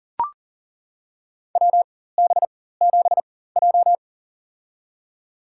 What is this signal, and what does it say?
Morse code, heard as a keyed tone, about 40 wpm, spaced as for 19 wpm, 700 Hz. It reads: WB7J
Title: USA Call Signs - 4 Characters - 4x Character Spacing - 40wpm